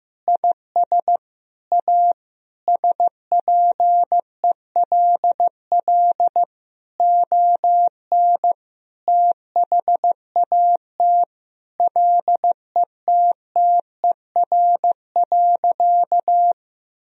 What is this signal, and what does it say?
Morse code, heard as a keyed tone, about 15 wpm, 700 Hz. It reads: IS A SPELL ON THAT LETTER.
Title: Sentences from Top 1000 Words - Review Part 1 - 15wpm